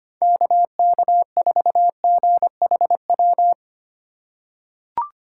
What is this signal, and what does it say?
Morse code, heard as a keyed tone, about 25 wpm, 700 Hz. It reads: KK4GHW
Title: USA Call Signs - 6 Characters - 3x Character Spacing - 25wpm